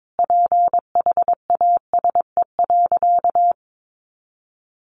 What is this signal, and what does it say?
Morse code, heard as a keyed tone, about 22 wpm, 700 Hz. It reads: PHASE.
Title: Sentences from Top 1200 Words 22wpm